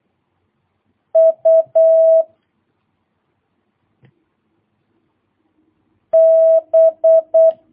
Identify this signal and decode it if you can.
Morse code, heard as a keyed tone, about 8 wpm, 650 Hz. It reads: U B
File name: jh8cbh / 2022cw